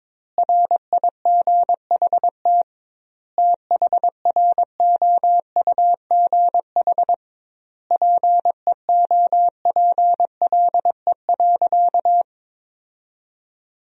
Morse code, heard as a keyed tone, about 22 wpm, 700 Hz. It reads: RIGHT THROUGH PEOPLE.
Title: 2x Word-Spacing - Sentences from Top 200 Words - 22wpm